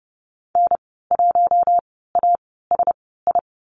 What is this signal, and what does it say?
Morse code, heard as a keyed tone, about 30 wpm, spaced as for 17 wpm, 700 Hz. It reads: N1ASI
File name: Lesson 15g - Number 5 - Call Signs 30wpm